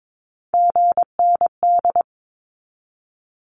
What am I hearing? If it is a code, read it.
Morse code, heard as a keyed tone, about 22 wpm, 700 Hz. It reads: GND
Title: Abbreviations 22wpm